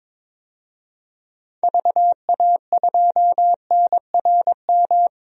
Morse code, heard as a keyed tone, about 22 wpm, 700 Hz. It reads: VA2NRM
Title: DXCC-100 Call Signs - 6 Characters - 22wpm